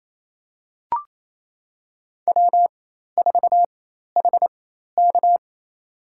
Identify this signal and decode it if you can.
Morse code, heard as a keyed tone, about 28 wpm, spaced as for 13 wpm, 700 Hz. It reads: W4HK
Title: USA Call Signs - 4 Characters - 4x Character Spacing - 28wpm